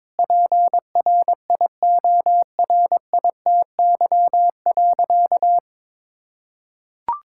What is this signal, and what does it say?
Morse code, heard as a keyed tone, about 22 wpm, 700 Hz. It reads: PRIORITY.
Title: Sentences from Top 1600 Words 22wpm